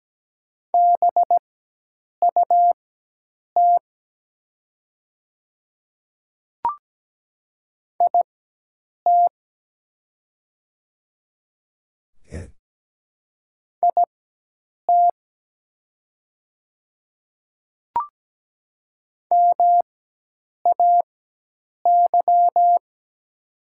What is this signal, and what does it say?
Morse code, heard as a keyed tone, about 17 wpm, spaced as for 8 wpm, 700 Hz. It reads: BUT IT IT MAY
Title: 4x Character Spacing - Top 100 Words - 17wpm